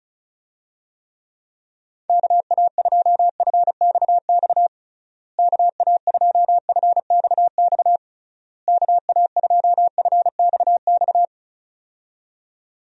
Morse code, heard as a keyed tone, about 36 wpm, 700 Hz. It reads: KA2FXX KA2FXX KA2FXX
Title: USA Call Signs - 6 Characters - Repeated 3x - 35wpm